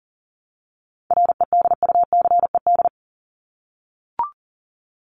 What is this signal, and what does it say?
Morse code, heard as a keyed tone, about 40 wpm, 700 Hz. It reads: REDUCED
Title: ED Suffix - Top 100 Words with Base Words 40wpm